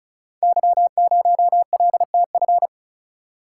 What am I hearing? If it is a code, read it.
Morse code, heard as a keyed tone, about 36 wpm, 700 Hz. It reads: Y0LTF
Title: Code Groups - 5 Letter:Numbers - 35wpm